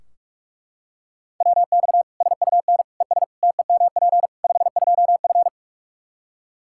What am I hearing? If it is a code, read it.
Morse code, heard as a keyed tone, about 41 wpm, 700 Hz. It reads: WX SUN ES TEMP 52F
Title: Practice QSO - Exchange 2 - Info - 45wpm